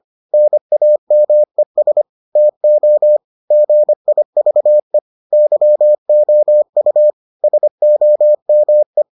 Morse code, heard as a keyed tone, about 25 wpm, 600 Hz. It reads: NAMES TO GIVE YOU SOME